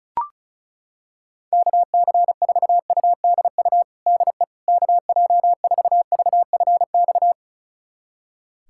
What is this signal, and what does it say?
Morse code, heard as a keyed tone, about 34 wpm, 700 Hz. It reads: KC4UDU DE KJ4VFX
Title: USA Callsign DE Callsign - 6 Characters - 35wpm